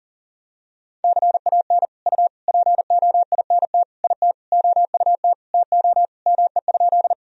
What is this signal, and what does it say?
Morse code, heard as a keyed tone, about 40 wpm, 700 Hz. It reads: CAN U POINT IT OUT TO ME?
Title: Sentences from Top 200 Words and Abbreviations 40wpm